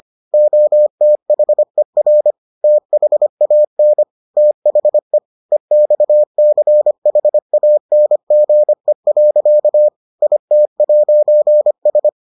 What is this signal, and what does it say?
Morse code, heard as a keyed tone, about 25 wpm, 600 Hz. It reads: OTHER THAN THE EXCHANGE. IT'S